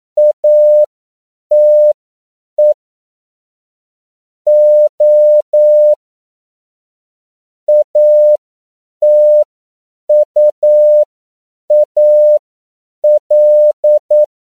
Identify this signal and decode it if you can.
Morse code, heard as a keyed tone, about 9 wpm, spaced as for 7 wpm, 600 Hz. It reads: ATE O ATUAL